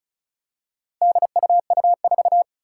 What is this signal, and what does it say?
Morse code, heard as a keyed tone, about 36 wpm, 700 Hz. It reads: DUU4
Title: Four Challenging Letter-Number Combinations 35wpm